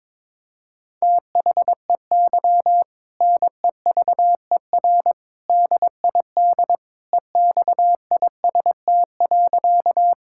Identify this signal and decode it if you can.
Morse code, heard as a keyed tone, about 21 wpm, 700 Hz. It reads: THEY NEVER DID EXIST.
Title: Sentences from Top 1000 Words - Review Part 5 - 22wpm